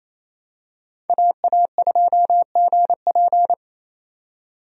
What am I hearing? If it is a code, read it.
Morse code, heard as a keyed tone, about 28 wpm, 700 Hz. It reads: AA2GP
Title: USA Call Signs - 5 Characters - 28wpm